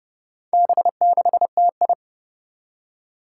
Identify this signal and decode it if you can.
Morse code, heard as a keyed tone, about 30 wpm, 700 Hz. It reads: B6TI
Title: Four Challenging Letter-Number Combinations 30wpm